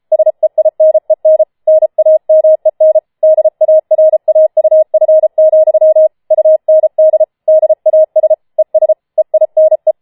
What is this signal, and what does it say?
Morse code, heard as a keyed tone, about 33 wpm, 630 Hz. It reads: SEINEN NAMEN DARAUF, UND DAS ES EINE
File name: Schneewittchen Teil5 34wpm